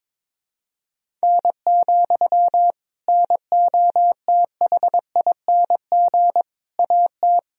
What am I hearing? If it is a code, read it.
Morse code, heard as a keyed tone, about 22 wpm, 700 Hz. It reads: N, NOTHING AT
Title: Sentences from Top 500 Words and Abbreviations 22wpm